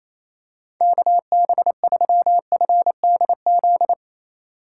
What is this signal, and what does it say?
Morse code, heard as a keyed tone, about 28 wpm, 700 Hz. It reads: KB3FDZ
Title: USA Call Signs - 6 Characters - 6x Character Spacing - 28wpm